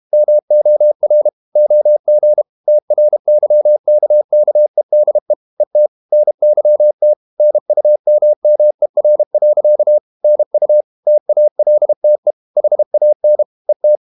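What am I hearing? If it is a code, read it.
Morse code, heard as a keyed tone, about 32 wpm, 600 Hz. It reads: MOR OG TRYKKEDE ET NYT NUMMER. NU TALTE HAN ET
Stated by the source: High-speed QRQ CW at 32 WPM.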